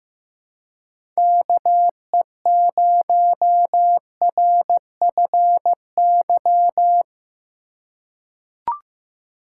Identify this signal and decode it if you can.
Morse code, heard as a keyed tone, about 15 wpm, 700 Hz. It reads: KE0RFY
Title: USA Call Signs - 6 Characters - 4x Character Spacing - 15wpm